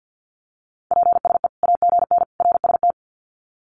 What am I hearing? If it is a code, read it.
Morse code, heard as a keyed tone, about 50 wpm, 700 Hz. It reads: PSE AGN RST
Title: Common QSO Phrases 50wpm